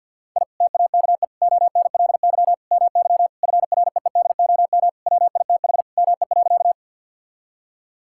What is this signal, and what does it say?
Morse code, heard as a keyed tone, about 50 wpm, 700 Hz. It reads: I TAKE ONLY MY FREEDOM WITH ME.
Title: Sentences from Top 1000 Words - Review Part 5 - 50wpm